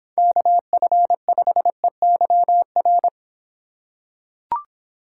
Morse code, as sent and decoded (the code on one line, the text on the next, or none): -.- ..-. ..... -.-- .-.
KF5EYR